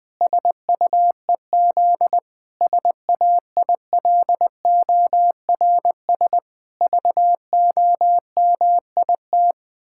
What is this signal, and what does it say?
Morse code, heard as a keyed tone, about 20 wpm, 700 Hz. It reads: SUEZ SAILORS VOMIT
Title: Pangram Challenge v2 - 20wpm